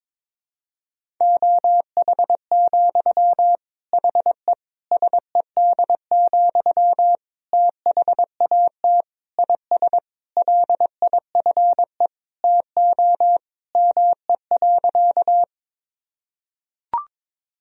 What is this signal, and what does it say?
Morse code, heard as a keyed tone, about 22 wpm, 700 Hz. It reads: OH, HE SED, THAT IS LIFE TO ME.
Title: Sentences from Top 500 Words and Abbreviations 22wpm